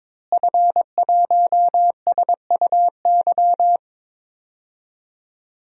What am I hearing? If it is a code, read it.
Morse code, heard as a keyed tone, about 22 wpm, 700 Hz. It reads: F1SUY
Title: DXCC-100 Call Signs - 5 Characters - Repeated 2x - 22wpm